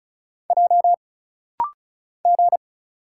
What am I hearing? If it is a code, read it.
Morse code, heard as a keyed tone, about 34 wpm, 700 Hz. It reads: J G
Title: Single Letters - Rapid-Fire 35wpm